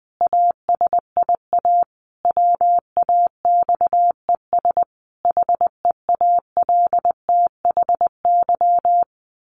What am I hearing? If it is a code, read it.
Morse code, heard as a keyed tone, about 19 wpm, 700 Hz. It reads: ASIA WAXES HEALTHY
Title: A Sound Of Thunder 20wpm